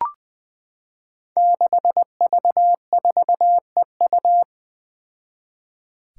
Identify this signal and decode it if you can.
Morse code, heard as a keyed tone, about 21 wpm, 700 Hz. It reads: 6V4EU
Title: Five Challenging Letter-Number Combinations 20wpm